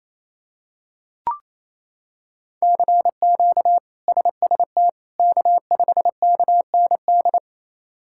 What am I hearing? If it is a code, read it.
Morse code, heard as a keyed tone, about 28 wpm, 700 Hz. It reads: CQ SST K5KND
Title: K1USN SST QSO Practice - 28wpm